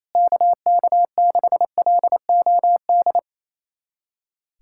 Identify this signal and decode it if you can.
Morse code, heard as a keyed tone, about 28 wpm, 700 Hz. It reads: KK6LOD
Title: USA Callsign DE Callsign - 6 Characters - 28wpm